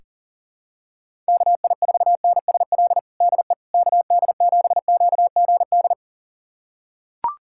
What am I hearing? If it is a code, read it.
Morse code, heard as a keyed tone, about 40 wpm, 700 Hz. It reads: KI4NSL DE KD7QGD
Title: USA Callsign DE Callsign - 6 Characters - 40wpm